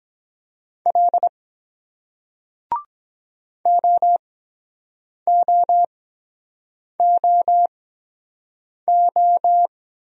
Morse code, heard as a keyed tone, about 23 wpm, spaced as for 7 wpm, 700 Hz. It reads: L OOOO